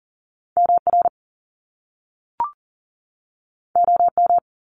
Morse code, heard as a keyed tone, about 39 wpm, 700 Hz. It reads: MR OM
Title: Top Two Letter Combinations in Words 40wpm